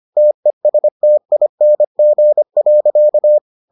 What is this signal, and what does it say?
Morse code, heard as a keyed tone, about 25 wpm, 600 Hz. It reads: TESTING.